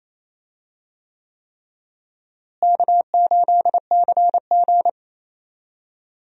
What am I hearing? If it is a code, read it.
Morse code, heard as a keyed tone, about 28 wpm, 700 Hz. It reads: K8CG
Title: USA Call Signs - 4 Characters - Repeated 3x - 28wpm